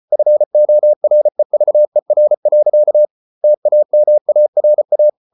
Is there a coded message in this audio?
Yes: Morse code from 0.1 to 5.1 s, heard as a keyed tone, about 34 wpm, 600 Hz. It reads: FOREVER. TAMARA